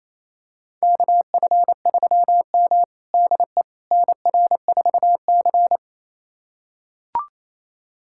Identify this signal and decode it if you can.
Morse code, heard as a keyed tone, about 28 wpm, 700 Hz. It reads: KF3M DE NR4C
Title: USA Callsign DE Callsign - 4 Characters - 28wpm